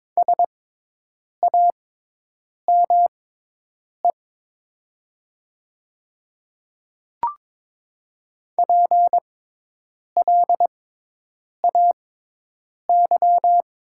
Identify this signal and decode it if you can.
Morse code, heard as a keyed tone, about 22 wpm, spaced as for 8 wpm, 700 Hz. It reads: SAME PLAY
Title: 6x Character Spacing - Top 200 Words - 22wpm